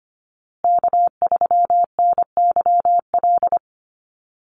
Morse code, heard as a keyed tone, about 25 wpm, 700 Hz. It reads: K3NYL
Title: USA Call Signs Repeated 2x 25wpm